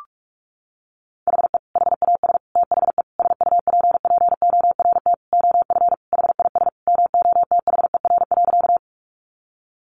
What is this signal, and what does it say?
Morse code, heard as a keyed tone, about 44 wpm, 700 Hz. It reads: HE HAS THE SUPPORT OF HIS MOTHER.
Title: Sentences from Top 800 Words 45wpm